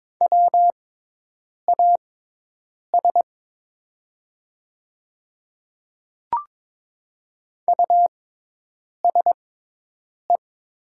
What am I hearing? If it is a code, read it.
Morse code, heard as a keyed tone, about 22 wpm, spaced as for 8 wpm, 700 Hz. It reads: WAS USE
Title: 6x Character Spacing - Top 100 Words - 22wpm